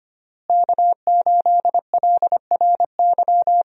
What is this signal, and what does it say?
Morse code, heard as a keyed tone, about 25 wpm, 700 Hz. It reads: K8LRY
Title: USA Call Signs - 5 Characters - Repeated 3x - 25wpm